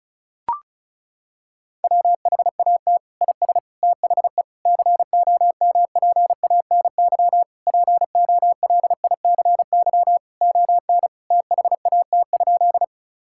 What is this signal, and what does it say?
Morse code, heard as a keyed tone, about 36 wpm, 700 Hz. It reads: WHAT IS THE COMPANY POLICY ON THAT?
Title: Sentences from Top 1000 Words - Review Part 5 - 35wpm